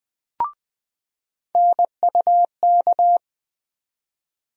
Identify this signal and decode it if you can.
Morse code, heard as a keyed tone, about 20 wpm, 700 Hz. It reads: NUK
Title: USA Call Sign Postfixes - 3 Letters 20wpm